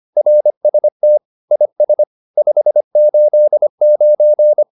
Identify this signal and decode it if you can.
Morse code, heard as a keyed tone, about 25 wpm, 600 Hz. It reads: RST IS 589